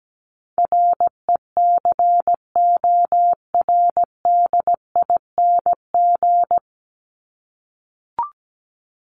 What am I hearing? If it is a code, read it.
Morse code, heard as a keyed tone, about 17 wpm, 700 Hz. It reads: RECORDING
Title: Top 1501-1750 Words 17wpm